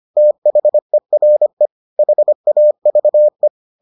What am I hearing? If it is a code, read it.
Morse code, heard as a keyed tone, about 25 wpm, 600 Hz. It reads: THERE HAVE